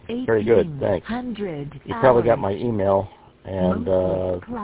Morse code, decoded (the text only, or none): none